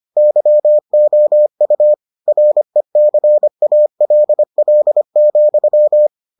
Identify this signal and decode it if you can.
Morse code, heard as a keyed tone, about 25 wpm, 600 Hz. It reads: YOU RECALL,